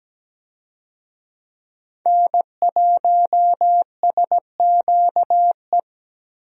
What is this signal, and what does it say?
Morse code, heard as a keyed tone, about 17 wpm, 700 Hz. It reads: N1SQE
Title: USA Call Signs - 5 Characters - Repeated 2x - 17wpm